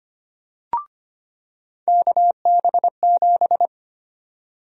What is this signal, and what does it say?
Morse code, heard as a keyed tone, about 25 wpm, 700 Hz. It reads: KB7
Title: USA Call Sign Prefixes - 25wpm